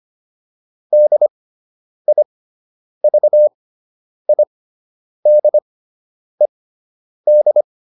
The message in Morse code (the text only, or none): DIVIDED